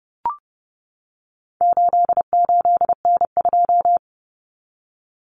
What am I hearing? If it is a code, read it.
Morse code, heard as a keyed tone, about 30 wpm, 700 Hz. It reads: 88N2
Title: Common Cut Numbers - 4-Digit 30wpm